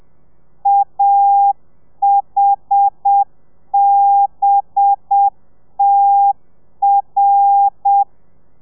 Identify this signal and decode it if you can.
Morse code, heard as a keyed tone, about 7 wpm, 800 Hz. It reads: AHBTR
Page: Lezione 1: audio a 30 CPM